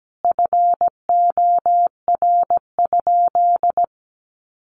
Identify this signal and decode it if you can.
Morse code, heard as a keyed tone, about 17 wpm, 700 Hz. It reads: FOR?
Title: Sentences from Top 400 Words 17wpm